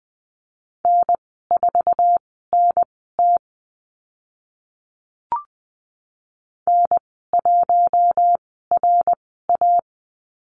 Morse code, heard as a keyed tone, about 20 wpm, spaced as for 14 wpm, 700 Hz. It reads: N4NT N1RA
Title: 062 Lesson 11g - Letter h - Call Signs 20wpm